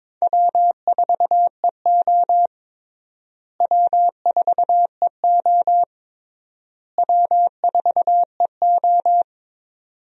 Morse code, heard as a keyed tone, about 22 wpm, 700 Hz. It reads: W4EO W4EO W4EO
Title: USA Call Signs - 4 Characters - Repeated 3x - 22wpm